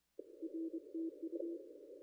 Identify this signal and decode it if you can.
Morse code, heard as a keyed tone, about 25 wpm, 350 Hz. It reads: RTU
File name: contests / ss-cw-k5tr-op-n5rz -0200